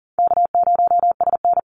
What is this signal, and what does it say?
Morse code, heard as a keyed tone, about 38 wpm, 700 Hz. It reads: K0SN
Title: Easy Call Signs - 40wpm